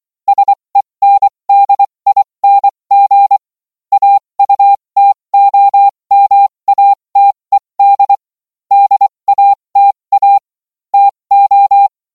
Morse code, heard as a keyed tone, about 24 wpm, spaced as for 20 wpm, 800 Hz. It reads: SENDING AUTOMATED DATA TO